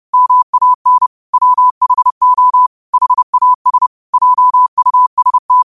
Morse code, heard as a keyed tone, about 30 wpm, 1000 Hz. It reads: MAN WHO HAS JUST
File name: qotdfiles / 30wpm